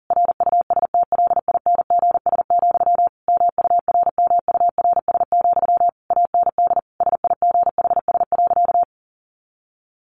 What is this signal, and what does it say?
Morse code, heard as a keyed tone, about 40 wpm, 700 Hz. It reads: RUSTLINGS, MURMURS, AND SIGHS.
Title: A Sound Of Thunder 40wpm